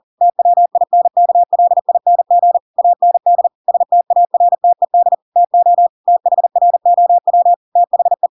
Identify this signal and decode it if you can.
Morse code, heard as a keyed tone, about 40 wpm, 700 Hz. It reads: TWINKLING AND STARTED TO THROW THE
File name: Pinocchio / Pinocchio ch7 40 40 ik1hge